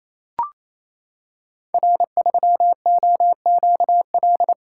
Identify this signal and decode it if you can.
Morse code, heard as a keyed tone, about 27 wpm, 700 Hz. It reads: R3OQL
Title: Code Groups - 5 Letter:Numbers - 28wpm